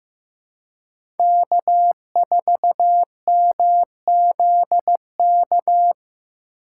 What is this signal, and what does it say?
Morse code, heard as a keyed tone, about 15 wpm, 700 Hz. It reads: K4MZK
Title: USA Call Signs - 5 Characters - 6x Character Spacing - 15wpm